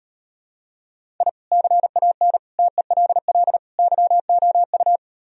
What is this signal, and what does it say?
Morse code, heard as a keyed tone, about 39 wpm, 700 Hz. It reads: I CAN TELL YOU